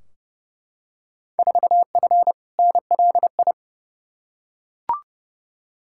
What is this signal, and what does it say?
Morse code, heard as a keyed tone, about 31 wpm, 700 Hz. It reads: 4F NLI
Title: Field Day Exchanges Encore - 30wpm